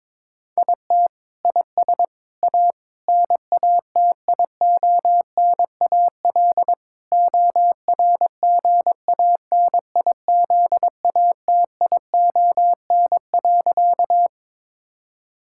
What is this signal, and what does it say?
Morse code, heard as a keyed tone, about 22 wpm, 700 Hz. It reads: IT IS A NATIONAL ORGANIZATION.
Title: Sentences from Top 1000 Words - Review Part 5 - 22wpm